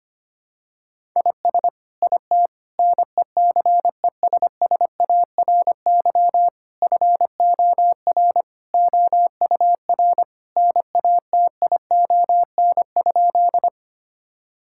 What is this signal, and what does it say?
Morse code, heard as a keyed tone, about 25 wpm, 700 Hz. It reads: IS IT NECESSARY FOR OUR NATION?
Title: Sentences from Top 1000 Words - Review Part 6 - 25wpm